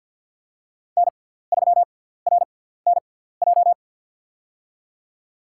Morse code, heard as a keyed tone, about 53 wpm, spaced as for 17 wpm, 700 Hz. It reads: N3RNJ
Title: USA Call Signs - 5 Characters - 6x Character Spacing - 50wpm